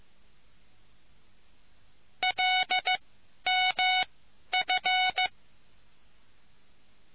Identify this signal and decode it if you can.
Morse code, heard as a keyed tone, about 15 wpm, spaced as for 11 wpm, 740 Hz. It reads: LMF